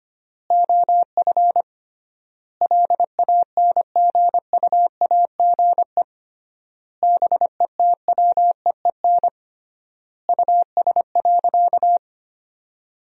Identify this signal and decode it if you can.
Morse code, heard as a keyed tone, about 25 wpm, 700 Hz. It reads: OF LANGUAGE BETWEEN US.
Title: 3x Word-Spacing - Sentences from Top 500 Words - 25wpm